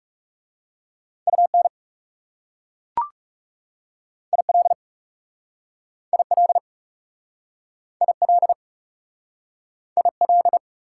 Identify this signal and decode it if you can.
Morse code, heard as a keyed tone, about 37 wpm, 700 Hz. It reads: UN IL IL IL IL
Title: Speed-Racing Prefixes and Suffixes 30wpm